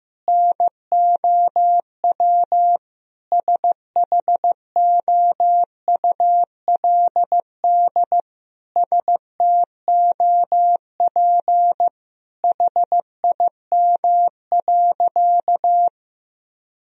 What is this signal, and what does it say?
Morse code, heard as a keyed tone, about 15 wpm, 700 Hz. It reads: NOW SHOULD STOP HIM.
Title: Sentences from Top 1000 Words - Review Part 1 - 15wpm